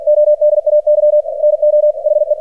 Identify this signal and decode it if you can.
Morse code, heard as a keyed tone, about 47 wpm, 600 Hz. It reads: ODAY NO HE
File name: archived / CRSnet